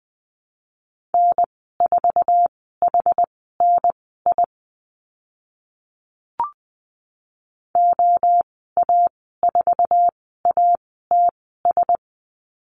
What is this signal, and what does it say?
Morse code, heard as a keyed tone, about 20 wpm, spaced as for 14 wpm, 700 Hz. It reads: N4HNI OA4ATS
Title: 062 Lesson 11g - Letter h - Call Signs 20wpm